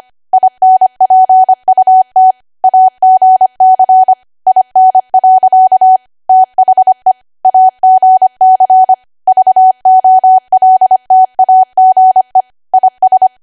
Morse code, heard as a keyed tone, about 25 wpm, 750 Hz. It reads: INPUT AGC IN. THE AGC VOLTAGE IS
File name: Archive / 25 WPM / 25WPM